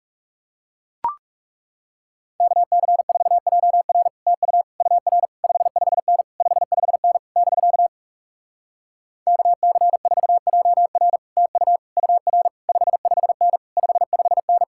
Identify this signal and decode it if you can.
Morse code, heard as a keyed tone, about 37 wpm, 700 Hz. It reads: KC4JR TU UR 55N 55N <BK> KC4JR TU UR 55N 55N
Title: Speed-Racing Common QSO Phrases - 30wpm